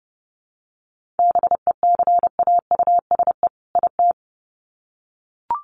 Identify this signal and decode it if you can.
Morse code, heard as a keyed tone, about 30 wpm, 700 Hz. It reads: BECAUSE IT
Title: Sets of 2 Words 30wpm